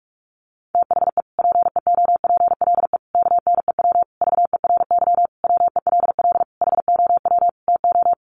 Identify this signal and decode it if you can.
Morse code, heard as a keyed tone, about 44 wpm, 700 Hz. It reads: THE PEOPLE KNEW VERY WELL HOW TO
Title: Sentences from Top 1100 Words 45wpm